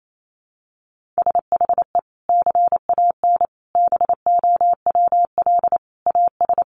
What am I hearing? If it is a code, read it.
Morse code, heard as a keyed tone, about 28 wpm, 700 Hz. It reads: SHE CAN BOWL AS
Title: Sentences from Top 1500 Words 28wpm